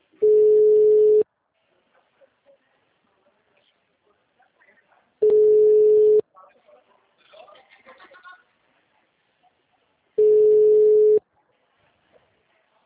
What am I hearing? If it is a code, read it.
Ringback tone (Europe, 425 Hz), with speech.
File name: out- -2010- - - .99182